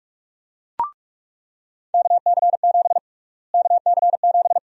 Morse code, heard as a keyed tone, about 43 wpm, 700 Hz. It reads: KC7 KC7
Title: USA Call Sign Prefixes - Repeated 2x - 45wpm